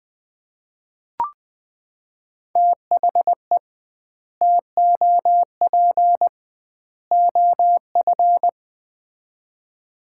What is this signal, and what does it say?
Morse code, heard as a keyed tone, about 20 wpm, 700 Hz. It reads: THE TOP OF
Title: 2x Word-Spacing - Sets of 3 Words - 20wpm